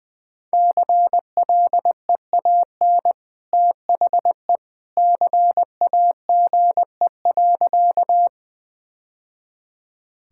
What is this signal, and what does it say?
Morse code, heard as a keyed tone, about 20 wpm, 700 Hz. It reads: CLEAN THE CAGE.
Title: Easy Sentences with Hints - 20wpm